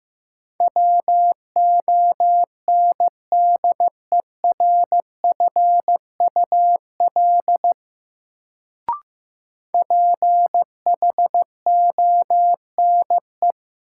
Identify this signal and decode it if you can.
Morse code, heard as a keyed tone, about 15 wpm, 700 Hz. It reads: WONDERFUL PHONE
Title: Sentences from Top 1100 Words - New Word Drill - 15wpm